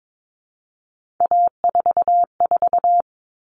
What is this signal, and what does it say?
Morse code, heard as a keyed tone, about 22 wpm, 700 Hz. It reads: A44
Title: All Cut Numbers - 3-Digit 22wpm